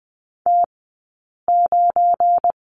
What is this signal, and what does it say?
Morse code, heard as a keyed tone, about 20 wpm, spaced as for 8 wpm, 700 Hz. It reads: T9
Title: 171 Lesson 28d - Number 9 - Two Characters 20wpm